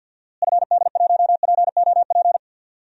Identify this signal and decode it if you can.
Morse code, heard as a keyed tone, about 47 wpm, 700 Hz. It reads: FD1POP
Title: DXCC-100 Call Signs - 6 Characters - Repeated 2x - 50wpm